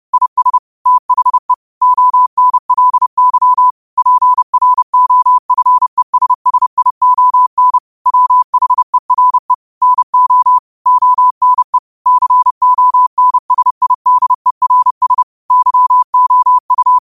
Morse code, heard as a keyed tone, about 30 wpm, 1000 Hz. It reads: IS THE ONLY PROFESSION WHERE NO ONE CONSIDERS YOU
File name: qotdfiles / 30wpm